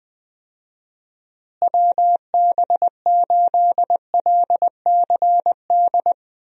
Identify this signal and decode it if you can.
Morse code, heard as a keyed tone, about 20 wpm, 700 Hz. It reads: WB8LCD
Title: USA Call Signs - 6 Characters - Repeated 2x - 20wpm